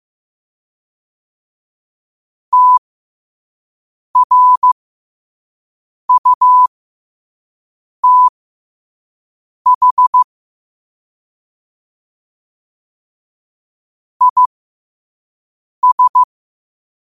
Morse code, heard as a keyed tone, about 15 wpm, spaced as for 5 wpm, 1000 Hz. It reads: TRUTH IS
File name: 5wpm